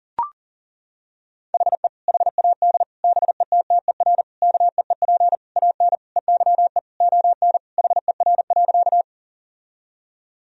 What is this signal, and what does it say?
Morse code, heard as a keyed tone, about 37 wpm, 700 Hz. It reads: HE HAD BETTER KEEP AN EYE ON HER.
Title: Sentences from Top 1000 Words - Review Part 1 - 40wpm